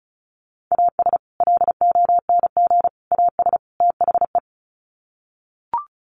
Morse code, heard as a keyed tone, about 34 wpm, 700 Hz. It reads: AS LONG AS THE
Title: Sets of 4 Words 35wpm